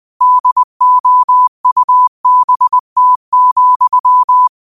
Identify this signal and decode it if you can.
Morse code, heard as a keyed tone, about 20 wpm, 1000 Hz. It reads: DOUBT,